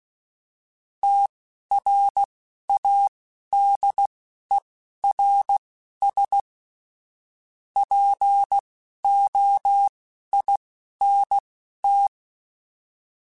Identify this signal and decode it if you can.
Morse code, heard as a keyed tone, about 16 wpm, spaced as for 11 wpm, 800 Hz. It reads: TRADERS POINT